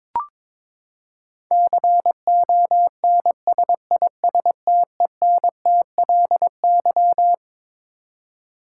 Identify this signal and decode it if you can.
Morse code, heard as a keyed tone, about 22 wpm, 700 Hz. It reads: CONSISTENTLY
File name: Twelve Letter Words - 22wpm